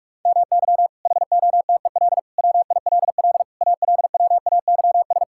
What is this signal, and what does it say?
Morse code, heard as a keyed tone, about 45 wpm, 700 Hz. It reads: MY HOTEL WILL ALWAYS
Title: Sentences from Top 1600 Words 45wpm